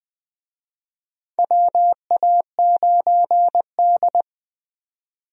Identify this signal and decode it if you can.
Morse code, heard as a keyed tone, about 20 wpm, 700 Hz. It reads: WA9D
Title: USA Call Signs - 4 Characters - 20wpm